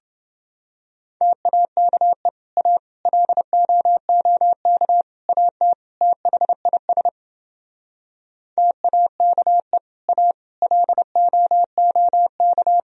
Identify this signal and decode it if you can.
Morse code, heard as a keyed tone, about 29 wpm, 700 Hz. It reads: TAKE A LOOK AT THIS TAKE A LOOK